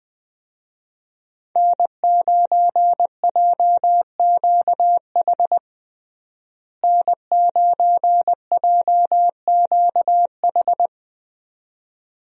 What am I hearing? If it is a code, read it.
Morse code, heard as a keyed tone, about 21 wpm, 700 Hz. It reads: N9JQH N9JQH
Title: USA Call Signs - 5 Characters - Repeated 2x - 20wpm